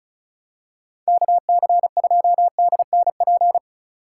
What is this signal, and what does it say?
Morse code, heard as a keyed tone, about 36 wpm, 700 Hz. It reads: KC2DNP
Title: USA Call Signs - 6 Characters - 3x Character Spacing - 35wpm